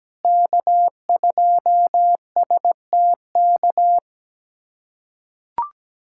Morse code, heard as a keyed tone, about 17 wpm, 700 Hz. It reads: K2STK
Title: USA Callsign DE Callsign - 5 Characters - 17wpm